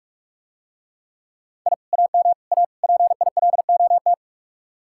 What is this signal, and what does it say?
Morse code, heard as a keyed tone, about 48 wpm, 700 Hz. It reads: I AM A PILOT